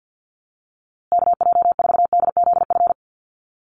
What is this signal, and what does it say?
Morse code, heard as a keyed tone, about 50 wpm, 700 Hz. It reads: KJ4DZF
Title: USA Call Signs Repeated 2x 50wpm